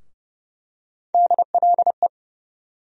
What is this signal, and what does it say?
Morse code, heard as a keyed tone, about 30 wpm, 700 Hz. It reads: DLE
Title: USA Call Sign Postfixes - 3 Letters 30wpm